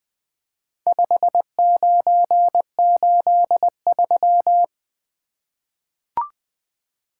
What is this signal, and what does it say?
Morse code, heard as a keyed tone, about 20 wpm, 700 Hz. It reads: 5983
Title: Four Letter-Number Combinations - 20wpm